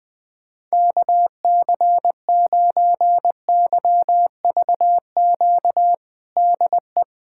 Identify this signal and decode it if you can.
Morse code, heard as a keyed tone, about 20 wpm, 700 Hz. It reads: KC9YVQ DE
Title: USA Callsign DE Callsign - 6 Characters - 20wpm